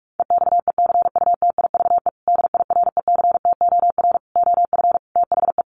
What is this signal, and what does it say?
Morse code, heard as a keyed tone, about 44 wpm, 700 Hz. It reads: EXECUTIVE DIRECTOR OF THE
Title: Sets of 5 Words Encore2 45wpm